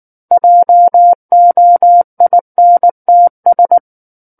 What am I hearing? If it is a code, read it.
Morse code, heard as a keyed tone, about 19 wpm, 700 Hz. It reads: JOINTS
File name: Book One / 10-Chapter 10 - 19 WPM